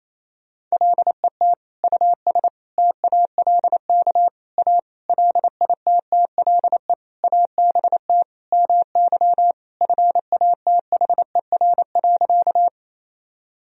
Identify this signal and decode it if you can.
Morse code, heard as a keyed tone, about 27 wpm, 700 Hz. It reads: LET US TALK A LITTLE ABT MY FATHER.
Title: Sentences from Top 400 Words and Abbreviations 28wpm